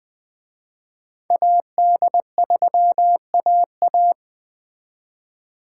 Morse code, heard as a keyed tone, about 20 wpm, 700 Hz. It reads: AD3AA